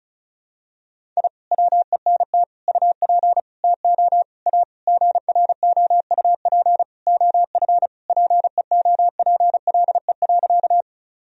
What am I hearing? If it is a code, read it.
Morse code, heard as a keyed tone, about 34 wpm, 700 Hz. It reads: I WENT UP TO A GROUP OF PEOPLE.
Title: Sentences from Top 1000 Words - Review Part 1 - 35wpm